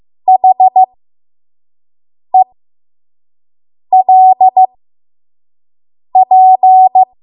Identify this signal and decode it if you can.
Morse code, heard as a keyed tone, about 15 wpm, spaced as for 5 wpm, 750 Hz. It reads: HELP
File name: Morse / 05WPM